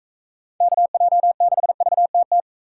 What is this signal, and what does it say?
Morse code, heard as a keyed tone, about 40 wpm, 700 Hz. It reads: KJ6VTT